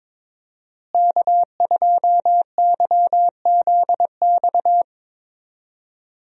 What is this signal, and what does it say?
Morse code, heard as a keyed tone, about 22 wpm, 700 Hz. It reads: K2YZX
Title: USA Call Signs - 5 Characters - Repeated 3x - 22wpm